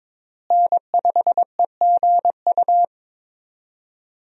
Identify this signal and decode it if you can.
Morse code, heard as a keyed tone, about 22 wpm, 700 Hz. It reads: N5EGU
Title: USA Call Signs - 5 Characters - Repeated 3x - 22wpm